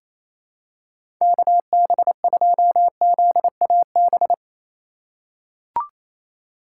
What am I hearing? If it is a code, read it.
Morse code, heard as a keyed tone, about 28 wpm, 700 Hz. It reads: KB2ZAB
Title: USA Call Signs - 6 Characters - 4x Character Spacing - 28wpm